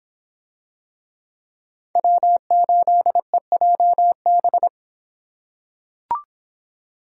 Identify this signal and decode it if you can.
Morse code, heard as a keyed tone, about 23 wpm, 700 Hz. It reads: W8EJB